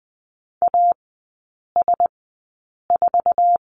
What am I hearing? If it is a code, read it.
Morse code, heard as a keyed tone, about 20 wpm, spaced as for 8 wpm, 700 Hz. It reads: AS4
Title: 107 Lesson 18e - Letter m - Three Characters 20wpm